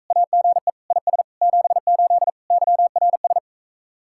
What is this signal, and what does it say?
Morse code, heard as a keyed tone, about 45 wpm, 700 Hz. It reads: AGE IS 78 YRS